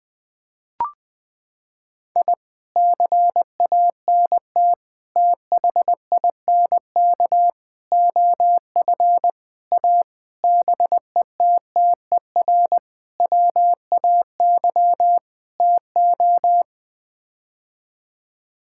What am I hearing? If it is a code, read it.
Morse code, heard as a keyed tone, about 20 wpm, 700 Hz. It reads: I CANT THINK OF A BETTER WAY TO
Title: Sets of 8 Words - 20wpm